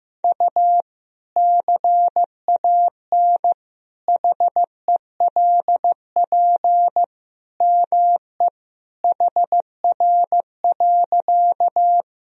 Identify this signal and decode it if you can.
Morse code, heard as a keyed tone, about 16 wpm, 700 Hz. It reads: U CAN HELP ME HR.
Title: Sentences from Top 200 Words and Abbreviations 15wpm